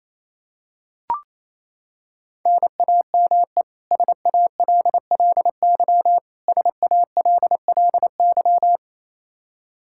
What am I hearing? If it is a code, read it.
Morse code, heard as a keyed tone, about 28 wpm, 700 Hz. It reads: NAME SALLY SALLY
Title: K1USN SST QSO Practice 28wpm